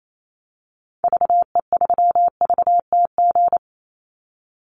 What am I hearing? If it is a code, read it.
Morse code, heard as a keyed tone, about 28 wpm, 700 Hz. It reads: VE3VTG
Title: Call Signs - 28wpm